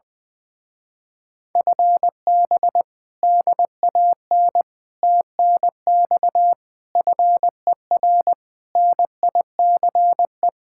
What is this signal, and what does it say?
Morse code, heard as a keyed tone, about 20 wpm, 700 Hz. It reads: FB DAN TNX FER NICE
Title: Speed-Racing Common QSO Phrases Encore - 20wpm